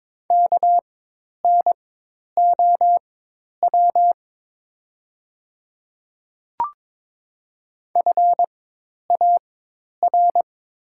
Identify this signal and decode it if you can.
Morse code, heard as a keyed tone, about 22 wpm, spaced as for 10 wpm, 700 Hz. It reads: KNOW FAR
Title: 4x Character Spacing - Top 300 Words - 22wpm